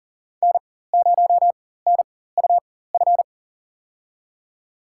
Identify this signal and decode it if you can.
Morse code, heard as a keyed tone, about 37 wpm, spaced as for 18 wpm, 700 Hz. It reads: N0NUF
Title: USA Call Signs - 4x Character Spacing - 40wpm